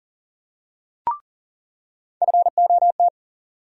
Morse code, heard as a keyed tone, about 40 wpm, 700 Hz. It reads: FOT
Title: USA Call Sign Postfixes - 3 Letters 40wpm